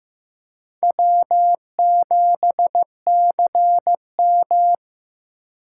Morse code, heard as a keyed tone, about 18 wpm, 700 Hz. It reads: W7CM
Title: Speed Racing - USA Call Signs 15wpm